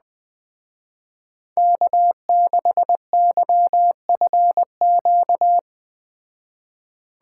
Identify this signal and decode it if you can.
Morse code, heard as a keyed tone, about 20 wpm, 700 Hz. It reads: K6YFQ
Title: Speed Racing - USA 1x3 Call Signs 17wpm